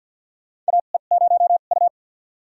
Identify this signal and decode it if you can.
Morse code, heard as a keyed tone, about 53 wpm, spaced as for 37 wpm, 700 Hz. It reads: AE0U
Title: USA Call Signs - 4 Characters - 2x Character Spacing - 50wpm